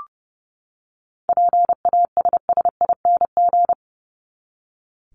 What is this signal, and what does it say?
Morse code, heard as a keyed tone, about 30 wpm, 700 Hz. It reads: PASSING
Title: ING Suffix - Top 100 Words with Base Words 30wpm